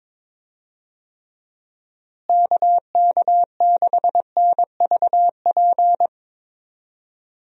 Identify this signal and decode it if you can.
Morse code, heard as a keyed tone, about 21 wpm, 700 Hz. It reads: KK6NVP
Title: USA Call Signs - 6 Characters - Repeated 3x - 22wpm